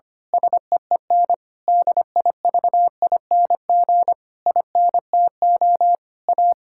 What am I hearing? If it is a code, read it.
Morse code, heard as a keyed tone, about 25 wpm, 700 Hz. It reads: SEEN DIVING INTO A
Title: Pangram Challenge v2 - 25wpm